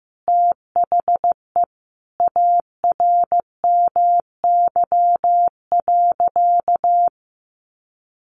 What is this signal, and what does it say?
Morse code, heard as a keyed tone, about 15 wpm, 700 Hz. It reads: THE ARMY.
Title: Sentences from Top 1400 Words 15wpm